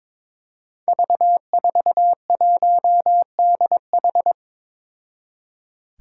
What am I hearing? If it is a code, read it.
Morse code, heard as a keyed tone, about 23 wpm, 700 Hz. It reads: V41DH
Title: Five Challenging Letter-Number Combinations 22wpm